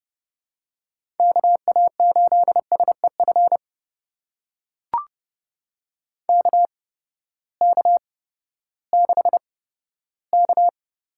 Morse code, heard as a keyed tone, about 31 wpm, spaced as for 15 wpm, 700 Hz. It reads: KA8SEF KK6K
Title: USA Call Signs - 6 Characters - 8x Character Spacing - 30wpm